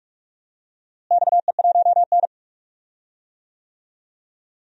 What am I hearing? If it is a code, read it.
Morse code, heard as a keyed tone, about 46 wpm, 700 Hz. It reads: XE1N
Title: DXCC-100 Call Signs - 4 Characters - Repeated 2x - 45wpm